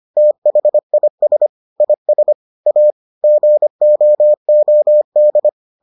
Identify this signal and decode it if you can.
Morse code, heard as a keyed tone, about 25 wpm, 600 Hz. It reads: THIS IS A GOOD